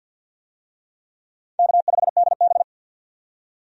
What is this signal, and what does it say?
Morse code, heard as a keyed tone, about 50 wpm, 700 Hz. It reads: K5DB
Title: USA Call Signs - 4 Characters - 50wpm